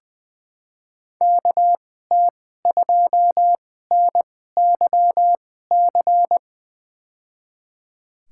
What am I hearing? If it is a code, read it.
Morse code, heard as a keyed tone, about 21 wpm, spaced as for 15 wpm, 700 Hz. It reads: KT2NYC
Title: USA Call Signs - 6 Characters - 2x Character Spacing - 20wpm